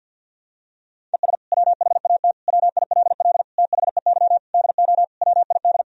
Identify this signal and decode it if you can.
Morse code, heard as a keyed tone, about 52 wpm, 700 Hz. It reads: ES WHAT WILL THEY DO WID
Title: Sentences from Top 200 Words and Abbreviations 50wpm